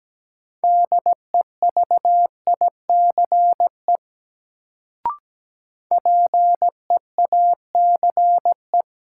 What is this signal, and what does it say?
Morse code, heard as a keyed tone, about 17 wpm, 700 Hz. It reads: DEVICE PEACE
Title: Sentences from Top 1500 Words - New Word Drill Alt - 17wpm